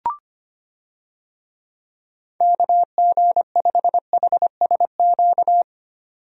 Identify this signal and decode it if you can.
Morse code, heard as a keyed tone, about 25 wpm, 700 Hz. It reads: KG5HSQ
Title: USA Call Signs - 6 Characters - 25wpm